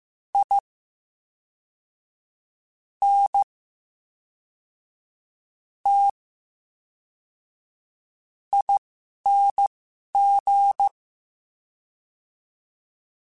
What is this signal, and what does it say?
Morse code, heard as a keyed tone, about 15 wpm, spaced as for 5 wpm, 800 Hz. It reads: INTING
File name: Special INGs / Special INGs-mp3s / VerySlowToFaster Ing @10